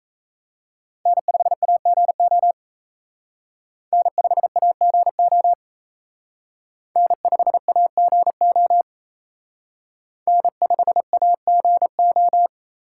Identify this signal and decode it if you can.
Morse code, heard as a keyed tone, about 34 wpm, 700 Hz. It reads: N5AGO N5AGO N5AGO N5AGO
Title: Speed Racing - USA 1x3 Call Signs 28wpm